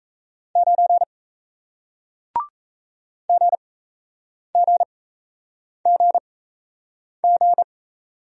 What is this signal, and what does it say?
Morse code, heard as a keyed tone, about 37 wpm, spaced as for 8 wpm, 700 Hz. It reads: 9 GGGG